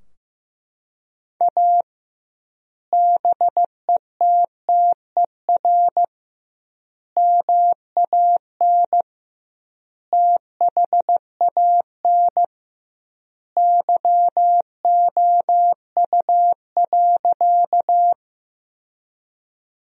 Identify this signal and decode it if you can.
Morse code, heard as a keyed tone, about 15 wpm, 700 Hz. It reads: A BETTER MAN THAN YOU.
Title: 2x Word-Spacing - Sentences from Top 400 Words - 15wpm